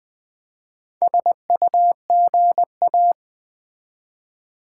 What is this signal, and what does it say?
Morse code, heard as a keyed tone, about 20 wpm, 700 Hz. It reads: SUGA
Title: Build A Word - 1 Letter At A Time - 5-Letter Words - 20wpm